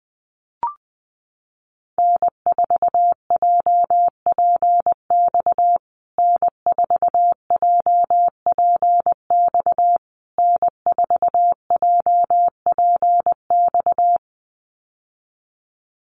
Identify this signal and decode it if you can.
Morse code, heard as a keyed tone, about 20 wpm, 700 Hz. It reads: N4JPX N4JPX N4JPX
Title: USA Call Signs Repeated 3x 20wpm